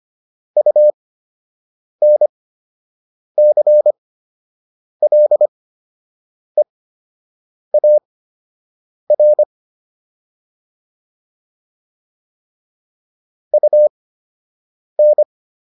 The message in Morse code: UNCLEAR UN

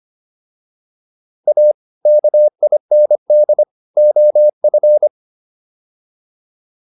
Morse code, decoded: A KIND OF

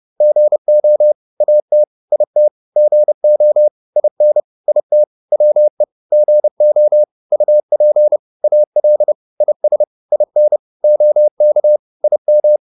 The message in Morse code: GO AT IT GO IN IT WE GO UP AL IS IN OK IM

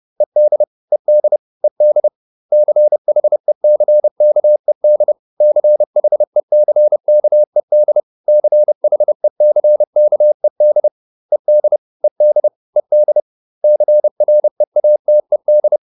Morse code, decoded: ED ED ED CHECKED CHECKED CHECKED ED ED ED CREATED

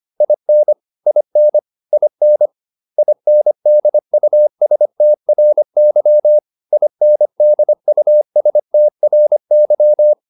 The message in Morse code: IN IN IN INDUSTRY INDUSTRY

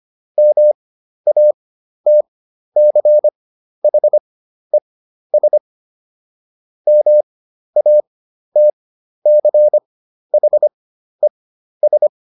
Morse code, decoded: MATCHES MATCHES

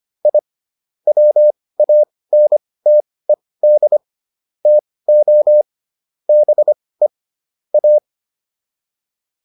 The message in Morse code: I WANTED TO BE A

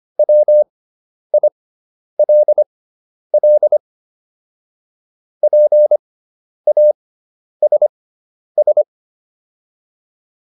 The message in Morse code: WILL PASS

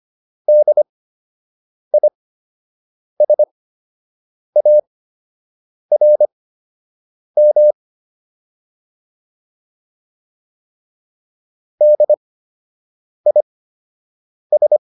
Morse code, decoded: DISARM DIS

